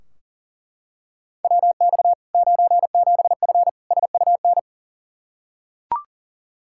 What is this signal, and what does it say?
Morse code, heard as a keyed tone, about 40 wpm, 700 Hz. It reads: WX 97F SUN
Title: Practice QSO - Exchange 2 - Info - 40wpm